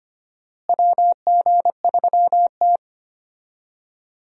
Morse code, heard as a keyed tone, about 25 wpm, 700 Hz. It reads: WG3T
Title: USA Call Signs - 4 Characters - Repeated 2x - 25wpm